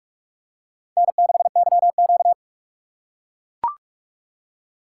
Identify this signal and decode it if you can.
Morse code, heard as a keyed tone, about 44 wpm, 700 Hz. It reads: N6YQ